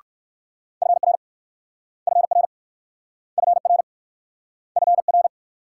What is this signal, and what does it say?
Morse code, heard as a keyed tone, about 56 wpm, 700 Hz. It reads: FR FR FR FR
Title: Speed-Racing Multi-letter Phonemes - 45wpm